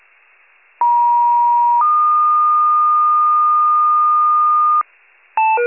Two-tone page, heard as a one-second tone, then a three-second tone, about 950 then 1220 Hz, with noise. It is heard from 0.8 to 4.8 s.
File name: wcalert